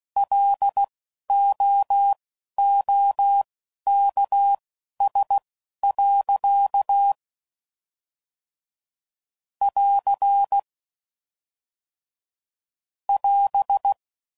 Morse code, heard as a keyed tone, about 16 wpm, spaced as for 11 wpm, 800 Hz. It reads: LOOKS. <AR> <AS>